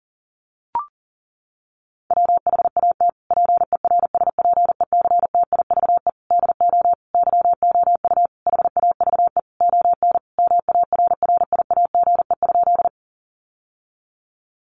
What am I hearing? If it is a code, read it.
Morse code, heard as a keyed tone, about 40 wpm, 700 Hz. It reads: WHAT PERSPECTIVE DO YOU HAVE ON MARRIAGE?
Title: Sentences from Top 1200 Words 40wpm